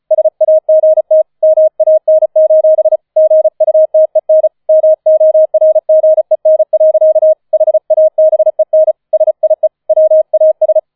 Morse code, heard as a keyed tone, about 36 wpm, 630 Hz. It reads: SAGT MAN: GUTEN MORGEN. HABEN SIE WAS
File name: 005 Fünfter Streich 35wpm